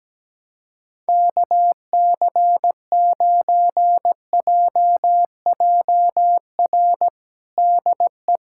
Morse code, heard as a keyed tone, about 17 wpm, 700 Hz. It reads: KC9JJR DE
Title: USA Callsign DE Callsign - 6 Characters - 17wpm